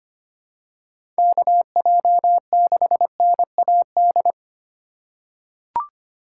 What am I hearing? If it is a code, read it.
Morse code, heard as a keyed tone, about 25 wpm, 700 Hz. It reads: KJ6NAD
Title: USA Call Signs - 4x Character Spacing - 25wpm